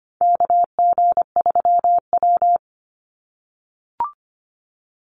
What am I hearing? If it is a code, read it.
Morse code, heard as a keyed tone, about 25 wpm, 700 Hz. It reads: KG3W
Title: NA SOTA Chaser Callsigns 25wpm